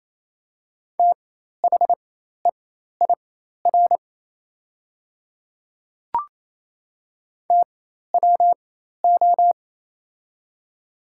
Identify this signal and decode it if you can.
Morse code, heard as a keyed tone, about 28 wpm, spaced as for 13 wpm, 700 Hz. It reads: THEIR TWO
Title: 4x Character Spacing - Top 100 Words - 28wpm